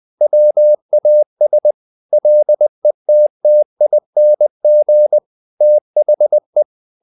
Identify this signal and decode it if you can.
Morse code, heard as a keyed tone, about 20 wpm, 600 Hz. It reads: WAS LETTING THE
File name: the old man and the sea / the old man and the sea part 1 20wpm